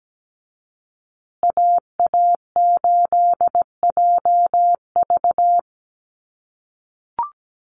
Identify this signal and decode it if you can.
Morse code, heard as a keyed tone, about 17 wpm, 700 Hz. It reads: AA8JV
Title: USA Call Signs Repeated 2x 17wpm